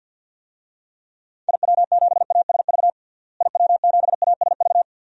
Morse code, heard as a keyed tone, about 51 wpm, 700 Hz. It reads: IW7ASV IW7ASV
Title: DXCC-100 Call Signs - 6 Characters - Repeated 2x - 50wpm